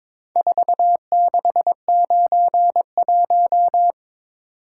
Morse code, heard as a keyed tone, about 22 wpm, 700 Hz. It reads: 4691
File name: practice / 22wpm / Four Letter-Number Combinations - 22wpm